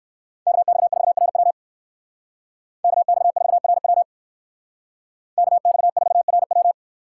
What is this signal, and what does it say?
Morse code, heard as a keyed tone, about 56 wpm, 700 Hz. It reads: KX4RW KX4RW KX4RW
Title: Speed Racing - USA 2x2 Call Signs 45wpm